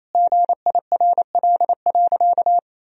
Morse code, heard as a keyed tone, about 28 wpm, 700 Hz. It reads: GIRL.
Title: Sentences from Top 400 Words and Abbreviations 28wpm